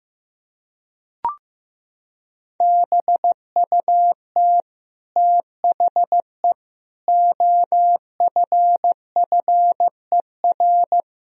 Morse code, heard as a keyed tone, about 15 wpm, 700 Hz. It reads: BUT THE OFFER
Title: Sentences from Top 1000 Words - Review Part 4 - 15wpm